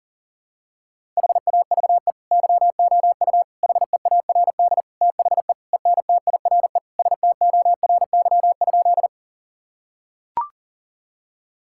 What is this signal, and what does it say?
Morse code, heard as a keyed tone, about 40 wpm, 700 Hz. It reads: HAVE YOU HEARD THE ENTIRE STORY?
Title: Sentences from Top 1000 Words - Review Part 5 - 40wpm